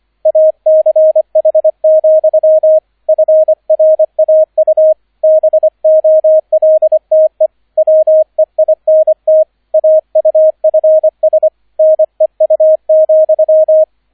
Morse code, heard as a keyed tone, about 25 wpm, 630 Hz, with radio noise. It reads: ACH, FRAU BOLTE WEINT AUFS NEU,